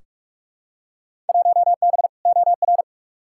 Morse code, heard as a keyed tone, about 46 wpm, 700 Hz. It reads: 1B OR
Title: Field Day Exchanges Encore - 45wpm